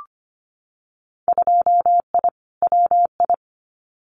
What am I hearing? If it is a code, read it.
Morse code, heard as a keyed tone, about 25 wpm, 700 Hz. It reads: 2I WI
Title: Winter Field Day Exchanges 25wpm